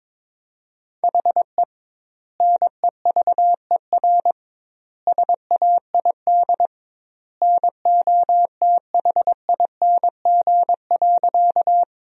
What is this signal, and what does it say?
Morse code, heard as a keyed tone, about 21 wpm, 700 Hz. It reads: HE NEVER SAID NOTHING.
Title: 2x Word-Spacing - Sentences from Top 500 Words - 22wpm